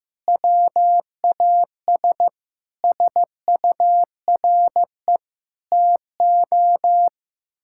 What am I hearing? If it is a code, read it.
Morse code, heard as a keyed tone, about 15 wpm, 700 Hz. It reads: WAS SURE TO
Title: Sentences from Top 1600 Words 15wpm